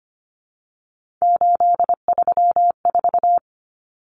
Morse code, heard as a keyed tone, about 25 wpm, 700 Hz. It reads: 834
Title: Common Cut Numbers - 3-Digit 25wpm